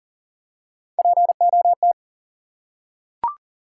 Morse code, heard as a keyed tone, about 40 wpm, 700 Hz. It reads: POT
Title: Three Letter Words - 40wpm